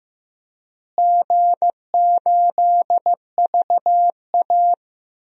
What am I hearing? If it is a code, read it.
Morse code, heard as a keyed tone, about 15 wpm, 700 Hz. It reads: G8VA
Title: DXCC-100 Call Signs - 4 Characters - Repeated 3x - 15wpm